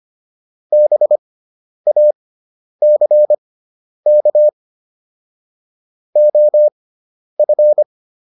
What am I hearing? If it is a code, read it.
Morse code, heard as a keyed tone, about 25 wpm, spaced as for 10 wpm, 600 Hz. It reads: BACK OF
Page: intermediate-practice-files